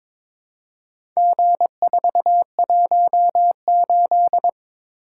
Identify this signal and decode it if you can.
Morse code, heard as a keyed tone, about 22 wpm, 700 Hz. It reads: G418
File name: practice / 22wpm / Four Letter-Number Combinations - 22wpm